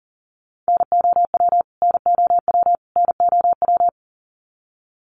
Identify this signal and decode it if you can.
Morse code, heard as a keyed tone, about 40 wpm, 700 Hz. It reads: NOW NOW NOW
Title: Top 300 Words Repeated 40wpm